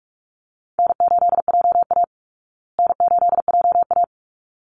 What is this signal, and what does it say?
Morse code, heard as a keyed tone, about 44 wpm, 700 Hz. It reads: N8JA N8JA
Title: Easy Call Signs Repeated 2x - 45wpm